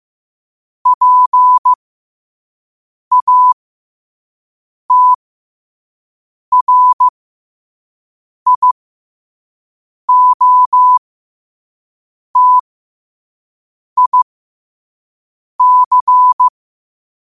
Morse code, heard as a keyed tone, about 15 wpm, spaced as for 5 wpm, 1000 Hz. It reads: PATRIOTIC
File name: qotdfiles / 5wpm